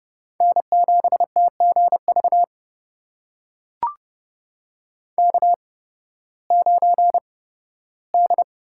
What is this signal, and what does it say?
Morse code, heard as a keyed tone, about 30 wpm, spaced as for 16 wpm, 700 Hz. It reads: N7TGV K9D
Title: USA Call Signs - 5 Characters - 8x Character Spacing - 30wpm